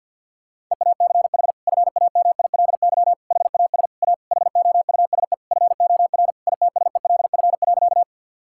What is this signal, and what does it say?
Morse code, heard as a keyed tone, about 50 wpm, 700 Hz. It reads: EACH FAMILY HAS A HOUSE FOR ITSELF.
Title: Sentences from Top 1000 Words - Review Part 5 - 50wpm